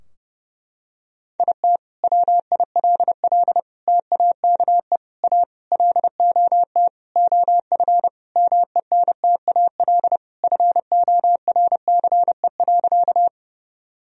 Morse code, heard as a keyed tone, about 32 wpm, 700 Hz. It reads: IT WILL TAKE A LOT OF MENTAL FORCE.
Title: Sentences from Top 1000 Words - Review Part 4 - 30wpm